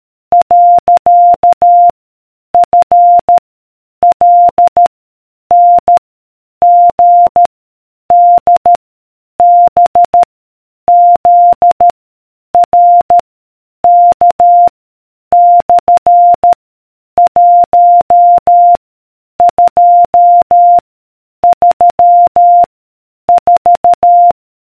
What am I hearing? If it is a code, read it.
Morse code, heard as a keyed tone, about 13 wpm, 700 Hz. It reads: . F L N G D B Z R K / 1 2 3 4